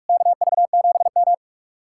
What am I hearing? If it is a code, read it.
Morse code, heard as a keyed tone, about 59 wpm, 700 Hz. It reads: KV7M